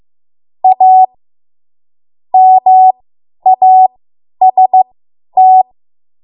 Morse code, heard as a keyed tone, about 15 wpm, spaced as for 10 wpm, 750 Hz. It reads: A MAST
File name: Morse / 10WPM